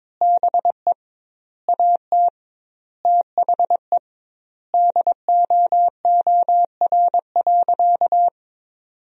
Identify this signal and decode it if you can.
Morse code, heard as a keyed tone, about 22 wpm, 700 Hz. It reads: BE AT THE DOOR.
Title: 2x Word-Spacing - Sentences from Top 400 Words - 22wpm